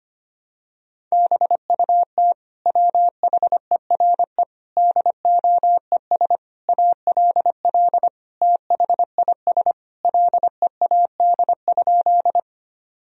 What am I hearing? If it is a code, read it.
Morse code, heard as a keyed tone, about 25 wpm, 700 Hz. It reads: BUT WHERE DOES ALL THIS LEAD?
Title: Sentences from Top 1000 Words - Review Part 2 - 25wpm